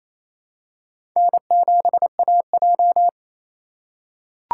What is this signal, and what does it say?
Morse code, heard as a keyed tone, about 28 wpm, 700 Hz. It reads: N7AJ
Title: USA Call Signs - 4 Characters - 3x Character Spacing - 28wpm